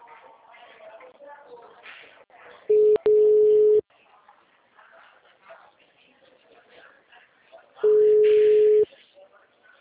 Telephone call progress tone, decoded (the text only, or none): ringback tone